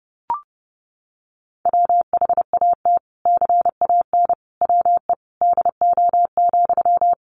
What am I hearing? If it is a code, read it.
Morse code, heard as a keyed tone, about 30 wpm, 700 Hz. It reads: WHAT CAN WE DO,
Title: Sentences from Top 100 Words 30wpm